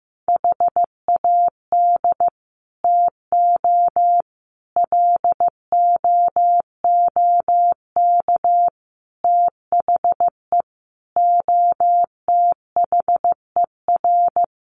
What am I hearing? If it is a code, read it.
Morse code, heard as a keyed tone, about 14 wpm, 700 Hz. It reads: HAD TO LOOK THE OTHER
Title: Sentences from Top 100 Words 15wpm